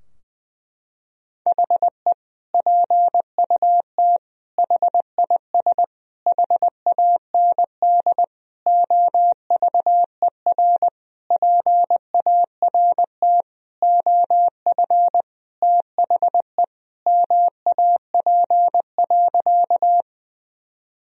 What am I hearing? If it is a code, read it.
Morse code, heard as a keyed tone, about 20 wpm, 700 Hz. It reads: HE PUT HIS HAND OVER PART OF THE MAP.
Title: Sentences from Top 1000 Words - Review Part 2 - 20wpm